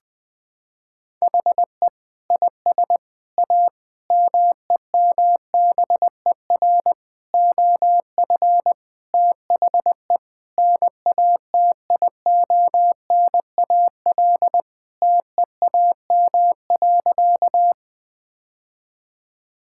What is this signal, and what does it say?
Morse code, heard as a keyed tone, about 20 wpm, 700 Hz. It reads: HE IS A MEMBER OF THE NATIONAL TEAM.
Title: Sentences with 3-Word Phrases - 20wpm